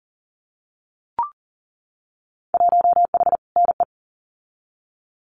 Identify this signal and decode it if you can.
Morse code, heard as a keyed tone, about 38 wpm, 700 Hz. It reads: 1H NE
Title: Winter Field Day Exchanges 40wpm